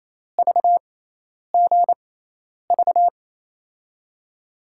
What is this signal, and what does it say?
Morse code, heard as a keyed tone, about 28 wpm, spaced as for 10 wpm, 700 Hz. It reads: VGV
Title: USA Call Signs - 5 Characters - 6x Character Spacing - 28wpm